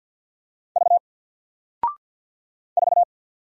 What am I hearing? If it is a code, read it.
Morse code, heard as a keyed tone, about 49 wpm, 700 Hz. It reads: V 4
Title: Single Character (Letter:Number:Punctuation:Prosign) Alt 50wpm